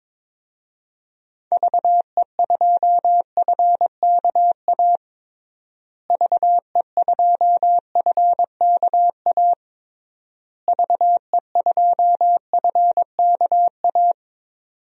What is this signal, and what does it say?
Morse code, heard as a keyed tone, about 22 wpm, 700 Hz. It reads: VE2FKA VE2FKA VE2FKA
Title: DXCC-100 Call Signs - 6 Characters - Repeated 3x - 22wpm